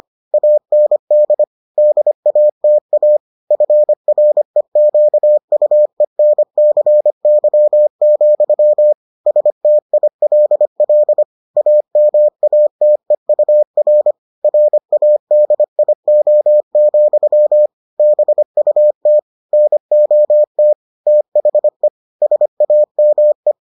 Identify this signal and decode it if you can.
Morse code, heard as a keyed tone, about 25 wpm, 600 Hz. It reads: AND DATA FREQUENCY, STILL AMATEUR RADIO, BUT NOT THE SAME